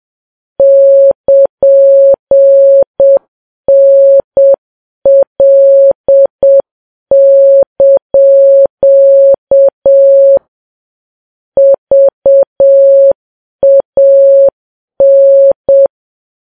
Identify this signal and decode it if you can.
Morse code, heard as a keyed tone, about 7 wpm, 550 Hz. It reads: <KN>NL) VAN
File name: Files Web tls / Morse on9fv / morse7kps